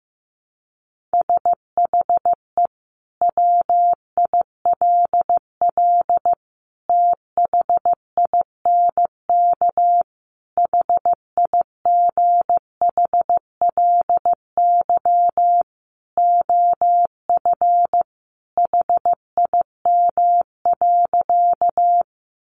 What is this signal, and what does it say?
Morse code, heard as a keyed tone, about 15 wpm, 700 Hz. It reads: SHE WILL THINK HIGHLY OF HIM.
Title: Sentences from Top 1100 Words 15wpm